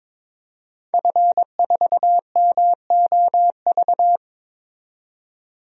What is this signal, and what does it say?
Morse code, heard as a keyed tone, about 22 wpm, 700 Hz. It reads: F4MOV
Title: DXCC-100 Call Signs - 5 Characters - 22wpm